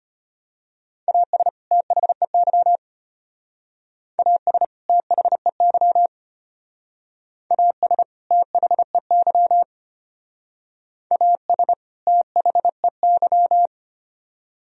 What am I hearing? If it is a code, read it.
Morse code, heard as a keyed tone, about 31 wpm, 700 Hz. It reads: AS THEY AS THEY AS THEY AS THEY
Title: Speed-Racing - Sets of 2 Words 25wpm